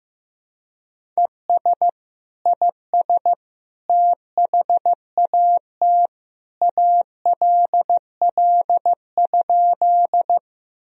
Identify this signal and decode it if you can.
Morse code, heard as a keyed tone, about 16 wpm, 700 Hz. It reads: ES IS THAT ALL?
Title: Sentences from Top 100 Words and Abbreviations 15wpm